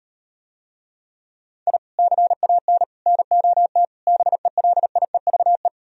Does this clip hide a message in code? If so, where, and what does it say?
Yes: Morse code from 1.7 to 5.7 s, heard as a keyed tone, about 38 wpm, 700 Hz. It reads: I CAN NOT BELIEVE